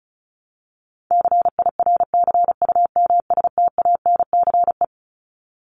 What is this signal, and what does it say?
Morse code, heard as a keyed tone, about 34 wpm, 700 Hz. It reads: CIRCUMSTANCE
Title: ES:S Suffix - Top 100 Words with Base Words 35wpm